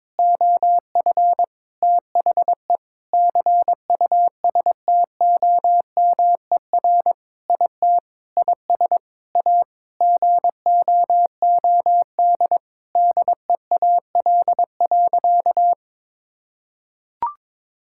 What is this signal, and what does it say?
Morse code, heard as a keyed tone, about 22 wpm, 700 Hz. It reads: OF THE CUSTOMER IT IS A GOOD DEAL.
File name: Sentences with 7-Word Phrases - 22wpm